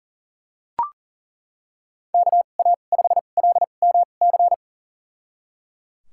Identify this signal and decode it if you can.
Morse code, heard as a keyed tone, about 38 wpm, spaced as for 28 wpm, 700 Hz. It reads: KA5LMC
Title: USA Call Signs - 6 Characters - 2x Character Spacing - 40wpm